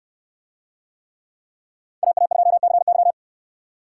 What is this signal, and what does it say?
Morse code, heard as a keyed tone, about 66 wpm, 700 Hz. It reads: AA2CY